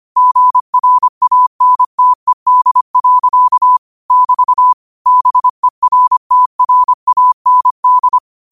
Morse code, heard as a keyed tone, about 25 wpm, 1000 Hz. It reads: GRANTED. <BT> BERTRAND
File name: qotdfiles / 25wpm